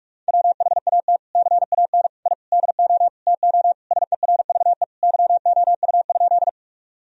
Morse code, heard as a keyed tone, about 46 wpm, 700 Hz. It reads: WHAT CAN I DO TO SERVE YOU?
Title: Sentences from Top 1000 Words - Review Part 2 - 45wpm